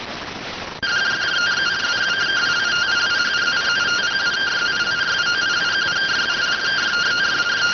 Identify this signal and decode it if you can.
Two-tone teleprinter signal, about 170 Hz shift.